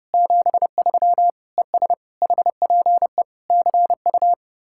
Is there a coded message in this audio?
Yes: Morse code from 0.1 to 4.3 s, heard as a keyed tone, about 40 wpm, 700 Hz. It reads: 73 ES HPE CU